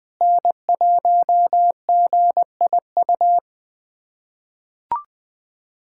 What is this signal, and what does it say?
Morse code, heard as a keyed tone, about 20 wpm, 700 Hz. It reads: N1GIU
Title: USA Callsign DE Callsign - 5 Characters - 20wpm